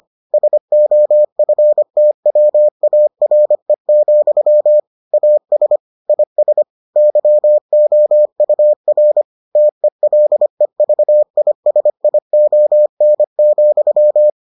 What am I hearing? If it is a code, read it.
Morse code, heard as a keyed tone, about 25 wpm, 600 Hz. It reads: SOFTWARE, AS IS YOUR TELEVISION,